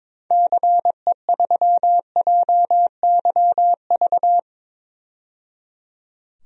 Morse code, heard as a keyed tone, about 22 wpm, 700 Hz. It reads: CE3JYV
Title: DXCC-100 Call Signs - 6 Characters - 22wpm